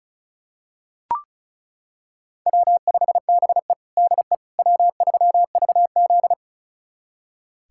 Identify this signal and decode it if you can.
Morse code, heard as a keyed tone, about 34 wpm, 700 Hz. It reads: W5BE DE W3VZ
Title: USA Callsign DE Callsign - 4 Characters - 35wpm